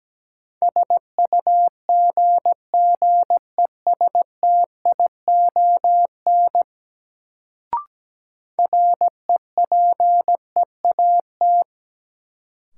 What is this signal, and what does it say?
Morse code, heard as a keyed tone, about 17 wpm, 700 Hz. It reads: SUGGESTION REPEAT
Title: Sentences from Top 1600 Words - New Word Drill Alt - 17wpm